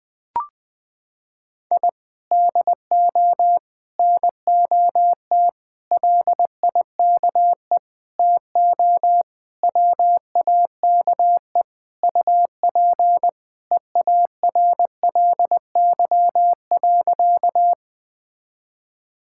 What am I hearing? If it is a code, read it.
Morse code, heard as a keyed tone, about 20 wpm, 700 Hz. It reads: I DO NOT LIKE TO WAKE UP EARLY.
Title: Sentences with 5-Word Phrases - 20wpm